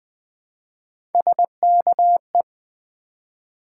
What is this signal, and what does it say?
Morse code, heard as a keyed tone, about 20 wpm, 700 Hz. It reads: SKE
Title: Build A Word - 3 Letters At A Time - 9-Letter Words - 20wpm